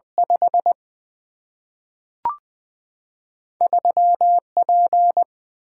Morse code, heard as a keyed tone, about 20 wpm, 700 Hz. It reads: 5 3P
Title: Single-Letter Single-Number 20wpm